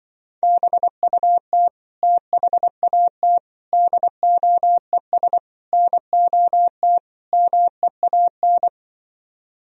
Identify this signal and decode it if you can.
Morse code, heard as a keyed tone, about 24 wpm, 700 Hz. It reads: BUT THAT DOES NOT MEAN
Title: Speed-Racing - Sets of 5 Words 20wpm